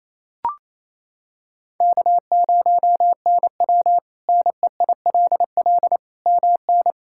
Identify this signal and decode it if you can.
Morse code, heard as a keyed tone, about 28 wpm, 700 Hz. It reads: K0NW NEILL MN
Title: K1USN SST QSO Practice - 28wpm